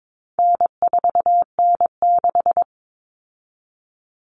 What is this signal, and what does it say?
Morse code, heard as a keyed tone, about 22 wpm, 700 Hz. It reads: N4N6
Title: Common Cut Numbers - 4-Digit 22wpm